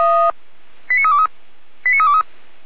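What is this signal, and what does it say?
Five-tone selective call (ZVEI), heard as quick runs of five tones, tones of about 70 ms. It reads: 89323 89323
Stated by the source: ZVEI five-tone call received by SDR, from the Feuerwehr Orlamünde (fire brigade) server.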